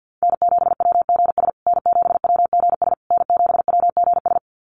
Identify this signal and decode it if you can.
Morse code, heard as a keyed tone, about 50 wpm, 700 Hz. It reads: N7WGS N7WGS N7WGS
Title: USA Call Signs Repeated 3x 50wpm